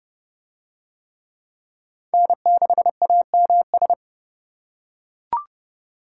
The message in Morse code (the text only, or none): N6AMS